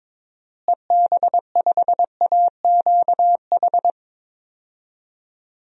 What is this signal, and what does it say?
Morse code, heard as a keyed tone, about 22 wpm, 700 Hz. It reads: EB5AQH
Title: DXCC-100 Call Signs - 6 Characters - Repeated 3x - 22wpm